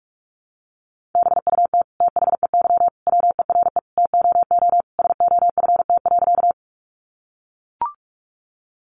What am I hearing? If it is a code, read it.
Morse code, heard as a keyed tone, about 44 wpm, 700 Hz. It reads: BUT THEY WERE TOO SOFT.
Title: Sentences from Top 1500 Words 45wpm